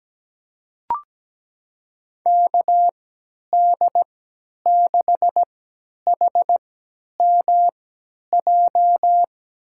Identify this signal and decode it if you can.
Morse code, heard as a keyed tone, about 17 wpm, spaced as for 10 wpm, 700 Hz. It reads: KD6HMJ
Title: USA Call Signs - 6 Characters - 3x Character Spacing - 17wpm